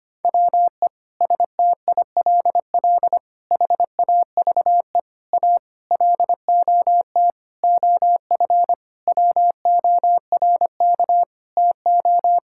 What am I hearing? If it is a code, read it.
Morse code, heard as a keyed tone, about 25 wpm, 700 Hz. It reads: WE STILL HAVE A LOT OF WORK TO
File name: practice / 25wpm / Sentences with 9-Word Phrases - 25wpm